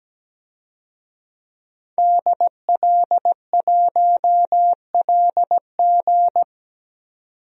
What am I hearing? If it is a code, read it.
Morse code, heard as a keyed tone, about 17 wpm, 700 Hz. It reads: DL1LG
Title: DXCC-100 Call Signs - 5 Characters - 17wpm